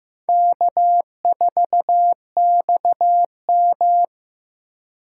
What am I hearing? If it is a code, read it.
Morse code, heard as a keyed tone, about 15 wpm, 700 Hz. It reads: K4XM
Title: USA Call Signs - 4 Characters - 2x Character Spacing - 15wpm